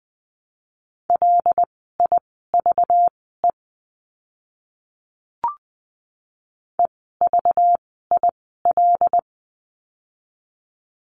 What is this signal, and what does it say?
Morse code, heard as a keyed tone, about 20 wpm, spaced as for 14 wpm, 700 Hz. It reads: LIVE EVIL
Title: 192 Lesson 31f - Letter v - Words 20wpm